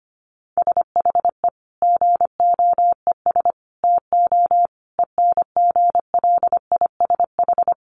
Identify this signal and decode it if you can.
Morse code, heard as a keyed tone, about 25 wpm, 700 Hz. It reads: SHE GOES TO ENGLISH
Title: Additional Easy Phrases 25wpm